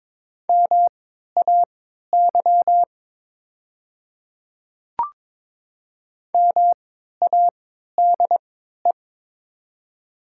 Morse code, heard as a keyed tone, about 22 wpm, spaced as for 12 wpm, 700 Hz. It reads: MAY MADE
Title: 3x Character Spacing - Top 300 Words - 22wpm